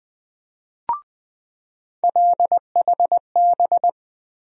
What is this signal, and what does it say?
Morse code, heard as a keyed tone, about 20 wpm, 700 Hz. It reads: LHB
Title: USA Call Sign Postfixes - 3 Letters 20wpm